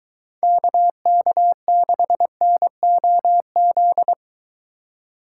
Morse code, heard as a keyed tone, about 23 wpm, 700 Hz. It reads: KK6NOZ